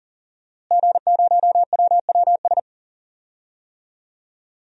Morse code, heard as a keyed tone, about 40 wpm, 700 Hz. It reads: G0WWS
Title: DXCC-100 Call Signs - 5 Characters - Repeated 2x - 40wpm